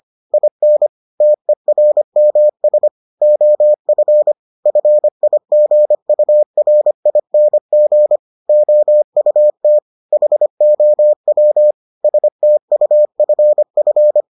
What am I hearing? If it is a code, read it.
Morse code, heard as a keyed tone, about 25 wpm, 600 Hz. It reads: IN TERMS OF FIGURING OUT HOW STUFF